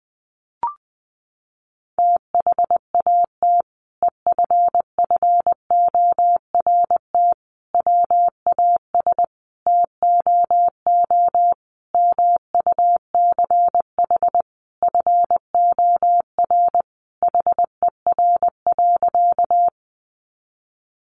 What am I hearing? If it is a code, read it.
Morse code, heard as a keyed tone, about 19 wpm, 700 Hz. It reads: THAT EFFORT WAS TOO MUCH FOR HER.
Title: Sentences from Top 1200 Words 20wpm